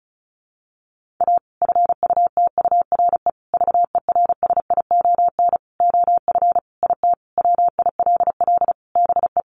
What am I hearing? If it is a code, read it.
Morse code, heard as a keyed tone, about 34 wpm, 700 Hz. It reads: A FUTURE VERSION OF IT WILL BE
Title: Sentences from Top 1400 Words 35wpm